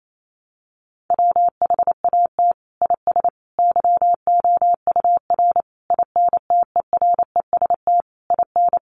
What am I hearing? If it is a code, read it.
Morse code, heard as a keyed tone, about 28 wpm, 700 Hz. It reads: WHAT IS YOUR INTEREST IN
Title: Sentences from Top 1300 Words 28wpm